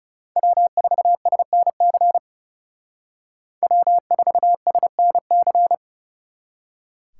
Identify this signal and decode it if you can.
Morse code, heard as a keyed tone, about 35 wpm, 700 Hz. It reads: W4SNC W4SNC
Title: Speed Racing - USA 1x3 Call Signs 30wpm